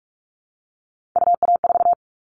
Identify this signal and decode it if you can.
Morse code, heard as a keyed tone, about 46 wpm, 700 Hz. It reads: UA4
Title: Callsign Prefixes for Top 25 DXCC - 45wpm